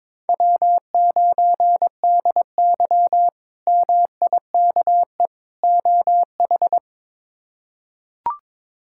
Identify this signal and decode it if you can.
Morse code, heard as a keyed tone, about 22 wpm, 700 Hz. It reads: W9DY MIKE OH
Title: K1USN SST QSO Practice - 22wpm